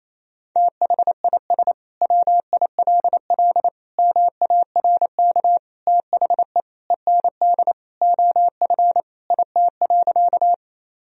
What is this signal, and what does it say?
Morse code, heard as a keyed tone, about 27 wpm, 700 Hz. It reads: THIS WILL MARK THE END OF IT.
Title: Sentences from Top 1000 Words - Review Part 1 - 28wpm